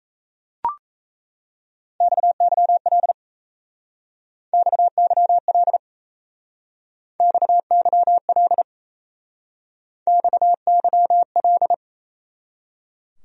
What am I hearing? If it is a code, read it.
Morse code, heard as a keyed tone, about 34 wpm, 700 Hz. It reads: XYL XYL XYL XYL
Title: Speed-Racing Abbreviations - 28wpm